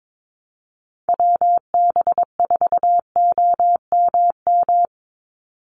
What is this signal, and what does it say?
Morse code, heard as a keyed tone, about 22 wpm, 700 Hz. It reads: WB4OMM
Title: Call Signs Repeated 3x - 22wpm